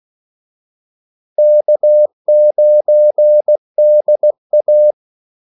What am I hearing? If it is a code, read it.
Morse code, heard as a keyed tone, about 16 wpm, 600 Hz. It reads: K9DA
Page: Kaldesignaler 16 wpm